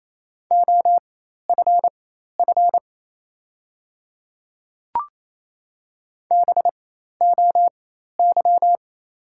Morse code, heard as a keyed tone, about 28 wpm, spaced as for 13 wpm, 700 Hz. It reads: OFF BOY
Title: 4x Character Spacing - Top 200 Words - 28wpm